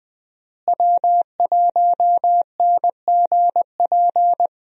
Morse code, heard as a keyed tone, about 20 wpm, 700 Hz. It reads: W1NGP
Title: USA Call Signs - 5 Characters - Repeated 3x - 20wpm